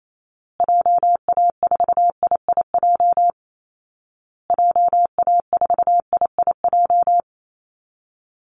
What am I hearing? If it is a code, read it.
Morse code, heard as a keyed tone, about 28 wpm, 700 Hz. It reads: JA4IIJ JA4IIJ
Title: Call Signs Repeated 2x - 28wpm